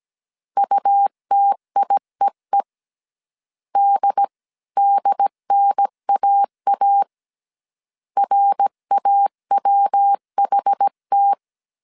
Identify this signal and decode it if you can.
Morse code, heard as a keyed tone, about 17 wpm, 800 Hz. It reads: UTIEE DDNAA RAWHT